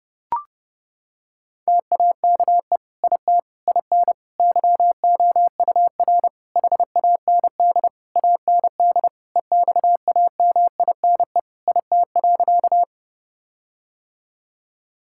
Morse code, heard as a keyed tone, about 30 wpm, 700 Hz. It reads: TAKE IT IN YOUR HAND AND EXAMINE IT.
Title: Sentences from Top 1600 Words 30wpm